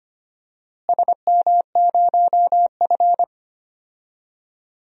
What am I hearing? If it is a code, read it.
Morse code, heard as a keyed tone, about 25 wpm, 700 Hz. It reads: SM0F
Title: DXCC-100 Call Signs - 4 Characters - Repeated 2x - 25wpm